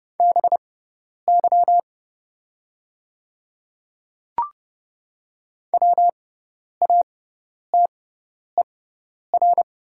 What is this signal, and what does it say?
Morse code, heard as a keyed tone, about 30 wpm, spaced as for 10 wpm, 700 Hz. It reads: BY WATER
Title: 6x Character Spacing - Top 100 Words - 30wpm